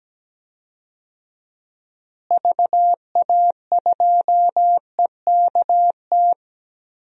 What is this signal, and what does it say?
Morse code, heard as a keyed tone, about 17 wpm, 700 Hz. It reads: VA2EKT
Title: DXCC-100 Call Signs - 6 Characters - Repeated 2x - 17wpm